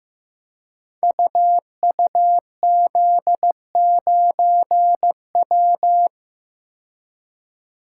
Morse code, heard as a keyed tone, about 15 wpm, 700 Hz. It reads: UUZ9W
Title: Code Groups - 5 Letter:Numbers - 15wpm